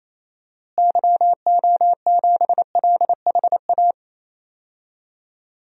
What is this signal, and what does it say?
Morse code, heard as a keyed tone, about 28 wpm, 700 Hz. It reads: YO7LHA
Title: DXCC-100 Call Signs - 6 Characters - Repeated 3x - 28wpm